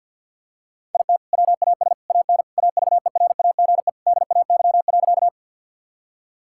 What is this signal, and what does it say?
Morse code, heard as a keyed tone, about 46 wpm, 700 Hz. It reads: IT WAS AN AVERAGE DAY.
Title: Sentences from Top 1000 Words - Review Part 6 - 50wpm